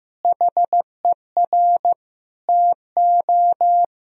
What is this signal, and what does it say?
Morse code, heard as a keyed tone, about 15 wpm, 700 Hz. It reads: HER TO
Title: Sentences from Top 1000 Words - Review Part 2 - 15wpm